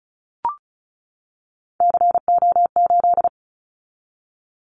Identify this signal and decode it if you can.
Morse code, heard as a keyed tone, about 34 wpm, 700 Hz. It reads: CO8
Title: Callsign Prefixes for Top 100 DXCC - 35wpm